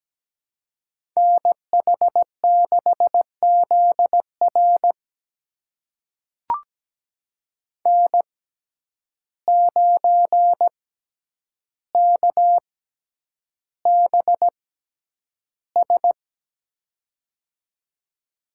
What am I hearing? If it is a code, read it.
Morse code, heard as a keyed tone, about 17 wpm, spaced as for 9 wpm, 700 Hz. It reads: NH6ZR N9KBS
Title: USA Call Signs - 5 Characters - 6x Character Spacing - 17wpm